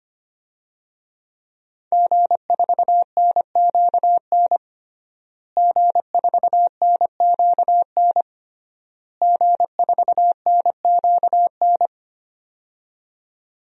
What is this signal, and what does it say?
Morse code, heard as a keyed tone, about 25 wpm, 700 Hz. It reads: G4NQN G4NQN G4NQN
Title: DXCC-100 Call Signs - 5 Characters - Repeated 3x - 25wpm